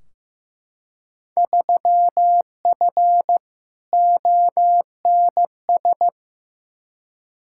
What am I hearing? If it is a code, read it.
Morse code, heard as a keyed tone, about 15 wpm, 700 Hz. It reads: 3F ONS
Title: Field Day Exchanges - 15wpm